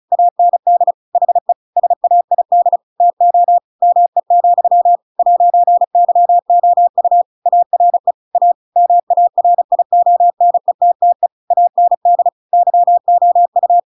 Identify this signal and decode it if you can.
Morse code, heard as a keyed tone, about 34 wpm, 700 Hz. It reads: AND HE SAID TO ME, 'YOU ARE A MARIONETTE AND YOU
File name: Pinocchio / Pinocchio ch7 35 35 ik1hge